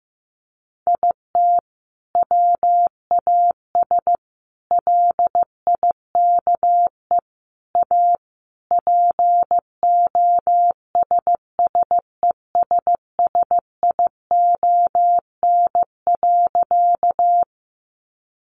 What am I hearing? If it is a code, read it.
Morse code, heard as a keyed tone, about 15 wpm, 700 Hz. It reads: IT WAS LIKE A POSSESSION.
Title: Sentences from Top 1400 Words 15wpm